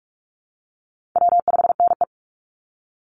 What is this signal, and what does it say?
Morse code, heard as a keyed tone, about 44 wpm, 700 Hz. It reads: W5NE
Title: Easy Call Signs Repeated 2x - 45wpm